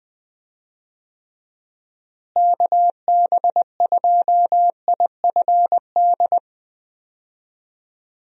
Morse code, heard as a keyed tone, about 20 wpm, 700 Hz. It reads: KB2IFD
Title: USA Call Signs - 6 Characters - Repeated 2x - 20wpm